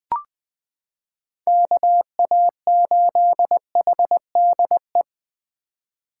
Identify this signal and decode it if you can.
Morse code, heard as a keyed tone, about 20 wpm, 700 Hz. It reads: KA8HDE
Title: K1USN SST QSO Practice - 20wpm